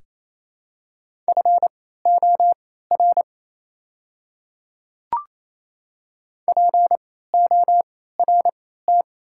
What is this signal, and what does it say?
Morse code, heard as a keyed tone, about 28 wpm, spaced as for 16 wpm, 690 Hz. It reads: FOR PORT
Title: 3x Character Spacing - Top 300 Words - 28wpm